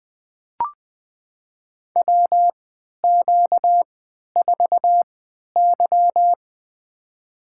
Morse code, heard as a keyed tone, about 20 wpm, spaced as for 11 wpm, 700 Hz. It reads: WQ4Y
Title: USA Call Signs - 4 Characters - 3x Character Spacing - 20wpm